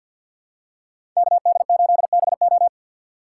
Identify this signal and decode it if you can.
Morse code, heard as a keyed tone, about 52 wpm, 700 Hz. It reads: KD8BO
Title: USA Call Signs - 5 Characters - 3x Character Spacing - 50wpm